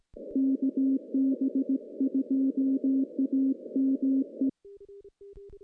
Morse code, heard as a keyed tone, about 17 wpm, 270 Hz. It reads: KB2AME
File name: CW-SS-k5tr